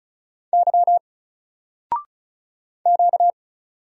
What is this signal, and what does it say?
Morse code, heard as a keyed tone, about 35 wpm, 700 Hz. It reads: Y Q
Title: Single Letter-Number 35wpm